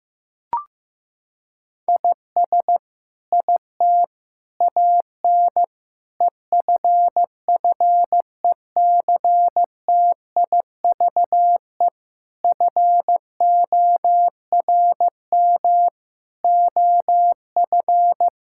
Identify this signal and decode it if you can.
Morse code, heard as a keyed tone, about 16 wpm, 700 Hz. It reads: IS IT AN EFFECTIVE FORM OF
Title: Sentences from Top 1000 Words - Review Part 6 - 15wpm